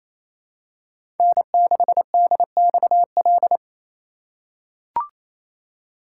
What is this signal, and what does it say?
Morse code, heard as a keyed tone, about 27 wpm, 700 Hz. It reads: N6DXL
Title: USA Call Signs - 5 Characters - 4x Character Spacing - 28wpm